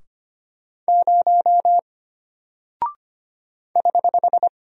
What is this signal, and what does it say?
Morse code, heard as a keyed tone, about 25 wpm, 700 Hz. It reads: 0 <HH>
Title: Single Character (Letter:Number:Punctuation:Prosign) Alt 25wpm